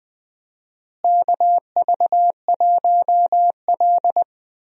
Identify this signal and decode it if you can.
Morse code, heard as a keyed tone, about 21 wpm, 700 Hz. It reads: KV1L
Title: USA Call Signs - 4 Characters - Repeated 3x - 20wpm